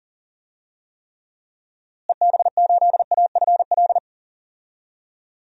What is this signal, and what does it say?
Morse code, heard as a keyed tone, about 40 wpm, 700 Hz. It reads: EB8AFL
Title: DXCC-100 Call Signs - 6 Characters - Repeated 2x - 40wpm